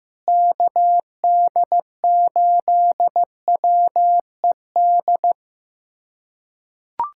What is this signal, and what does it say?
Morse code, heard as a keyed tone, about 16 wpm, 700 Hz. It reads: KD8WED
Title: USA Callsign DE Callsign - 6 Characters - 15wpm